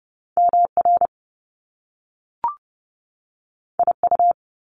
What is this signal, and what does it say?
Morse code, heard as a keyed tone, about 30 wpm, 700 Hz. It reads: MR IU
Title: Top Two Letter Combinations in Words 30wpm